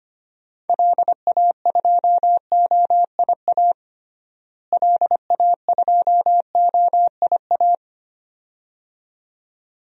Morse code, heard as a keyed tone, about 25 wpm, 700 Hz. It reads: LA2OIA LA2OIA
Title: DXCC-100 Call Signs - 6 Characters - Repeated 2x - 25wpm